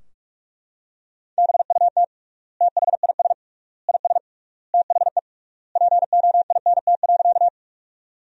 Morse code, heard as a keyed tone, about 42 wpm, 700 Hz. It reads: BUT THIS IS THE POINT.
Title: 3x Word-Spacing - Sentences from Top 200 Words - 45wpm